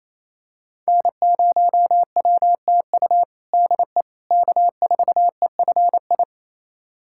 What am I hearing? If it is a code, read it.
Morse code, heard as a keyed tone, about 28 wpm, 700 Hz. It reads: N0WTU DE K4EFI
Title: USA Callsign DE Callsign - 5 Characters - 28wpm